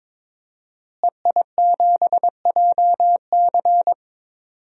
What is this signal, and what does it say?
Morse code, heard as a keyed tone, about 22 wpm, 700 Hz. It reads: EI7JC
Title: DXCC-100 Call Signs - 5 Characters - Repeated 2x - 22wpm